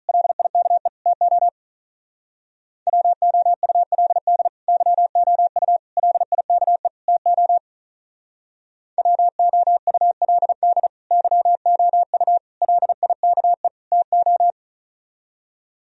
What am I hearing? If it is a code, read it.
Morse code, heard as a keyed tone, about 44 wpm, 700 Hz. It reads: LIKE TO WOULD YOU LIKE TO WOULD YOU LIKE TO